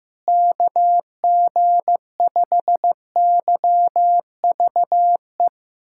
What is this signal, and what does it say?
Morse code, heard as a keyed tone, about 15 wpm, 700 Hz. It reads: KG5YVE
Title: USA Callsign DE Callsign - 6 Characters - 15wpm